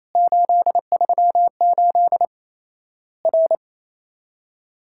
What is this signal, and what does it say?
Morse code, heard as a keyed tone, about 28 wpm, 700 Hz. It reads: 838 R
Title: Worked All Europe DX Contest - QTC Exchange - 28wpm